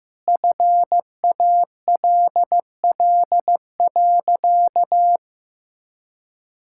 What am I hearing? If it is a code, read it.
Morse code, heard as a keyed tone, about 15 wpm, 700 Hz. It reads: FALL.
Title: Sentences from Top 1000 Words - Review Part 6 - 15wpm